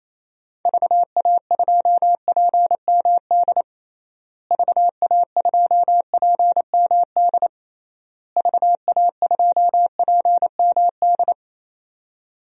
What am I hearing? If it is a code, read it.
Morse code, heard as a keyed tone, about 28 wpm, 700 Hz. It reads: VA2PMD VA2PMD VA2PMD
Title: DXCC-100 Call Signs - 6 Characters - Repeated 3x - 28wpm